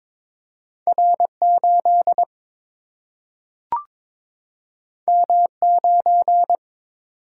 Morse code, heard as a keyed tone, about 22 wpm, 700 Hz. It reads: R8 M9
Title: Single-Letter Single-Number 22wpm